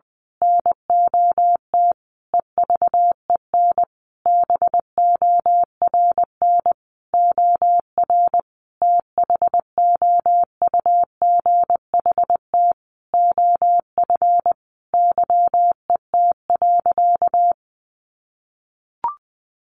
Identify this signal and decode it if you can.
Morse code, heard as a keyed tone, about 20 wpm, 700 Hz. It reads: NOT EVEN BORN OR THOUGHT OF YET.
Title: A Sound Of Thunder 20wpm